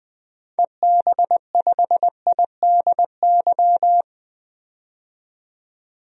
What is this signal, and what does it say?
Morse code, heard as a keyed tone, about 20 wpm, 700 Hz. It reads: EB5IDY
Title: DXCC-100 Call Signs - 6 Characters - 20wpm